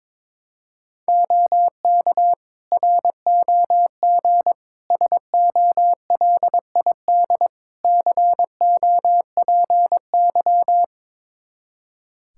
Morse code, heard as a keyed tone, about 22 wpm, 700 Hz. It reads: OK ROG SOLID COPY
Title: Practice QSO - Exchange 3 - More Info - 22wpm